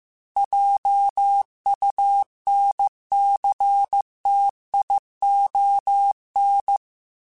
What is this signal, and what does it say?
Morse code, heard as a keyed tone, about 15 wpm, 800 Hz. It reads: JUNCTION